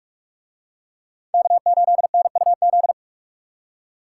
Morse code, heard as a keyed tone, about 44 wpm, 700 Hz. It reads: K8NUZ